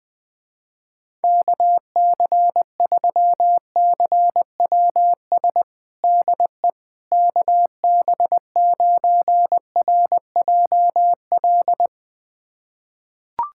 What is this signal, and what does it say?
Morse code, heard as a keyed tone, about 21 wpm, 700 Hz. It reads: KC3CWS DE KB9RJL
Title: USA Callsign DE Callsign - 6 Characters - 20wpm